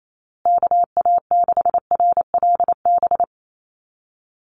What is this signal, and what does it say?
Morse code, heard as a keyed tone, about 28 wpm, 700 Hz. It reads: KA6RLB
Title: USA Call Signs Repeated 3x 28wpm